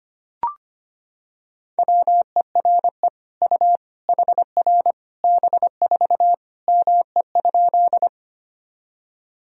Morse code, heard as a keyed tone, about 25 wpm, 700 Hz. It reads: WERE U HR B4 ME?
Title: Sentences from Top 200 Words and Abbreviations 25wpm